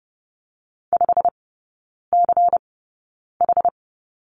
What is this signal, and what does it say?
Morse code, heard as a keyed tone, about 30 wpm, spaced as for 9 wpm, 700 Hz. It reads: 5CH
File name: Lesson 16e - Letter c - Three Characters 30wpm